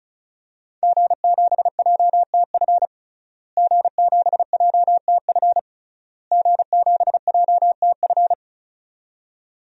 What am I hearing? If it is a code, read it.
Morse code, heard as a keyed tone, about 36 wpm, 700 Hz. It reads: G7JTF G7JTF G7JTF
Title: DXCC-100 Call Signs - 5 Characters - Repeated 3x - 35wpm